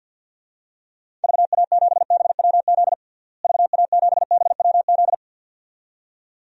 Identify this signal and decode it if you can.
Morse code, heard as a keyed tone, about 53 wpm, 700 Hz. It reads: VA7BWZ VA7BWZ
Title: DXCC-100 Call Signs - 6 Characters - Repeated 2x - 50wpm